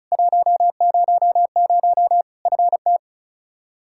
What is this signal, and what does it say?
Morse code, heard as a keyed tone, about 36 wpm, 700 Hz. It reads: 100 FT
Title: Practice QSO - Exchange 2 - Info - 35wpm